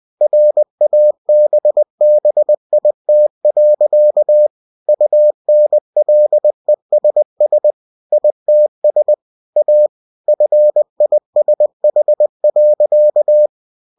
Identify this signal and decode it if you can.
Morse code, heard as a keyed tone, about 20 wpm, 600 Hz. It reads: RABBIT. UNLESS ITS A FISH.
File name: cw-books / winnie-the-pooh / winnie-20wpm-page-0069